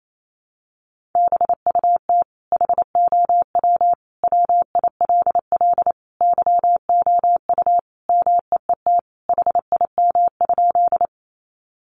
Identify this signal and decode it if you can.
Morse code, heard as a keyed tone, about 28 wpm, 700 Hz. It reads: BUT HOW WILL YOU MEET HIM?
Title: Sentences from Top 1100 Words 28wpm